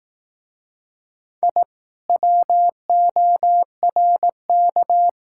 Morse code, heard as a keyed tone, about 21 wpm, 700 Hz. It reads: I WORK